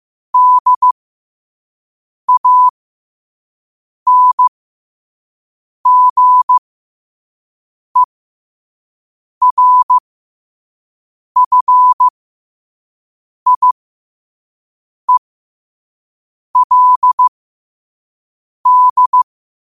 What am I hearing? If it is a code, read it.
Morse code, heard as a keyed tone, about 15 wpm, spaced as for 5 wpm, 1000 Hz. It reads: DANGERFIELD
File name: qotdfiles / 5wpm